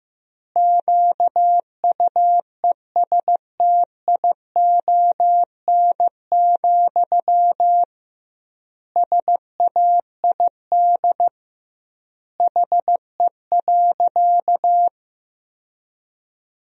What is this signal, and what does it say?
Morse code, heard as a keyed tone, about 16 wpm, 700 Hz. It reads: QUESTION, SAID HE.
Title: 2x Word-Spacing - Sentences from Top 400 Words - 15wpm